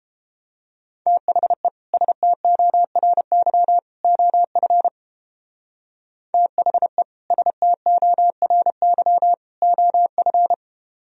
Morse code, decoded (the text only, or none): THE STORY OF THE STORY OF